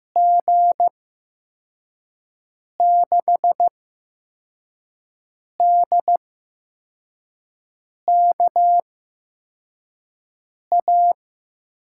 Morse code, heard as a keyed tone, about 15 wpm, spaced as for 4 wpm, 700 Hz. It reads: G6DKA
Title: USA Call Signs - 6 Characters - 8x Character Spacing - 15wpm